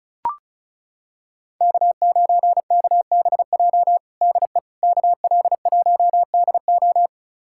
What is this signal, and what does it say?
Morse code, heard as a keyed tone, about 34 wpm, 700 Hz. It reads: K9KBJ DE KL1DO
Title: USA Callsign DE Callsign - 5 Characters - 35wpm